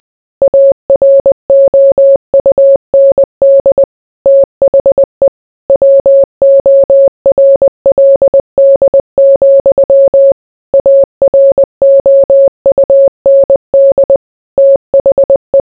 Morse code, heard as a keyed tone, about 20 wpm, 550 Hz. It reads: AROUND THE WORLD, AROUND THE